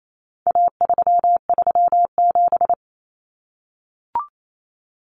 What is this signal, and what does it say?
Morse code, heard as a keyed tone, about 28 wpm, 700 Hz. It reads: A337
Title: Common Cut Numbers - 4-Digit 28wpm